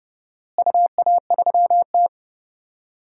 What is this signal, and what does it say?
Morse code, heard as a keyed tone, about 30 wpm, 700 Hz. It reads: UA3T
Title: DXCC-100 Call Signs - 4 Characters - Repeated 2x - 30wpm